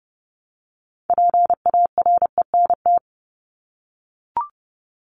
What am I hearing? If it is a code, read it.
Morse code, heard as a keyed tone, about 30 wpm, 700 Hz. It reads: PARENT
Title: Top 1751-2000 Words 30wpm